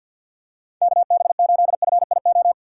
Morse code, heard as a keyed tone, about 48 wpm, 700 Hz. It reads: KB8FIO
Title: USA Call Signs - 6 Characters - 4x Character Spacing - 50wpm